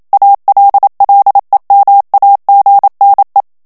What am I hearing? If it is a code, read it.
Morse code, heard as a keyed tone, about 27 wpm, 800 Hz. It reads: ALLEMAGNE
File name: CW / FICHIERS-MP3 / FRANCAIS TEXTE CLAIR 1 VITESSE 26 MOTS MINUTES